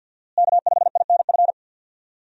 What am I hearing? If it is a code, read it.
Morse code, heard as a keyed tone, about 51 wpm, 700 Hz. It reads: K5INF
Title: USA Call Signs - 5 Characters - Repeated 3x - 50wpm